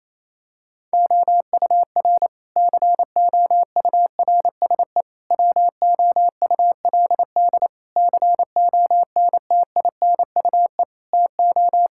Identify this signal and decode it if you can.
Morse code, heard as a keyed tone, about 28 wpm, 700 Hz. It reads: OUR COURSE WOULD CONTINUE TO
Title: Sentences from Top 1000 Words - Review Part 6 - 28wpm